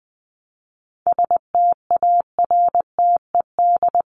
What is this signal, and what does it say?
Morse code, heard as a keyed tone, about 20 wpm, 700 Hz. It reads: STARTED
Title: ED Suffix - Top 100 Words with Base Words 20wpm